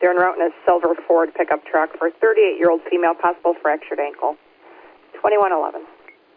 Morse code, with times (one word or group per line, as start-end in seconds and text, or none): none